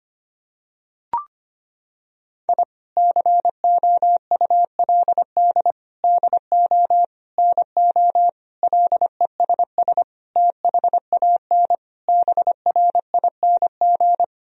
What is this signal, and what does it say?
Morse code, heard as a keyed tone, about 25 wpm, 700 Hz. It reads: I COULD DO NO LESS THAN BRING
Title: Sentences from Top 1000 Words - Review Part 3 - 25wpm